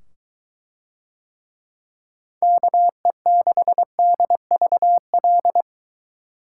Morse code, heard as a keyed tone, about 23 wpm, 700 Hz. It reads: KE6DVL